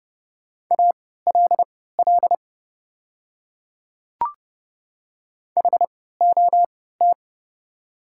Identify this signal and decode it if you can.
Morse code, heard as a keyed tone, about 30 wpm, spaced as for 17 wpm, 700 Hz. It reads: ALL HOT
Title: 3x Character Spacing - Top 200 Words - 30wpm